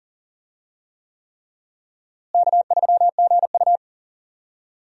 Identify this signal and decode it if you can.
Morse code, heard as a keyed tone, about 38 wpm, 700 Hz. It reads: K3GU
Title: USA Call Signs - 4 Characters - Repeated 3x - 40wpm